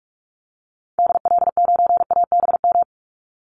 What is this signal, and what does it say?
Morse code, heard as a keyed tone, about 44 wpm, 700 Hz. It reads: DL9ABM
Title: Call Signs - 45wpm